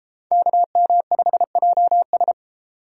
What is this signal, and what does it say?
Morse code, heard as a keyed tone, about 33 wpm, 700 Hz. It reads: KM5JS